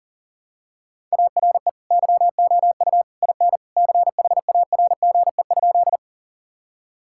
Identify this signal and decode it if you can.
Morse code, heard as a keyed tone, about 40 wpm, 700 Hz. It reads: ARE YOU IN CHARGE?
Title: Sentences from Top 1000 Words - Review Part 6 - 40wpm